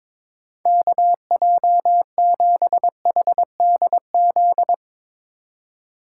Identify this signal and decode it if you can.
Morse code, heard as a keyed tone, about 22 wpm, 700 Hz. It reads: KJ7HDZ
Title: Speed Racing - USA Call Signs 22wpm